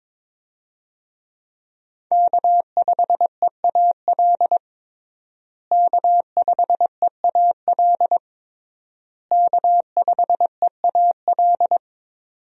Morse code, heard as a keyed tone, about 22 wpm, 700 Hz. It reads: K5EAL K5EAL K5EAL
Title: USA Call Signs - 5 Characters - Repeated 3x - 22wpm